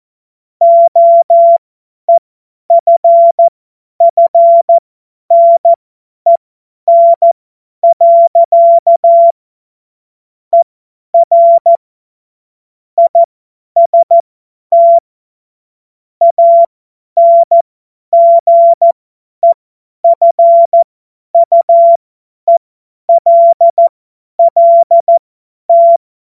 Morse code, heard as a keyed tone, about 14 wpm, spaced as for 10 wpm, 680 Hz. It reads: OEFFNEN. ER IST ANGEFUELLT
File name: Morsekurs / Witze / Witz 28. März 2023 50 F70